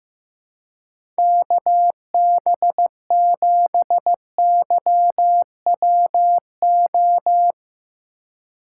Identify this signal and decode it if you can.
Morse code, heard as a keyed tone, about 15 wpm, 700 Hz. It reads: KB7YWO
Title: USA Call Signs - 4x Character Spacing - 15wpm